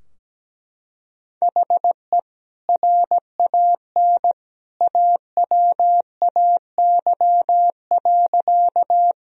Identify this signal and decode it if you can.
Morse code, heard as a keyed tone, about 17 wpm, 700 Hz. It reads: HE RAN AWAY.
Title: Sentences from Top 1000 Words - Review Part 3 - 17wpm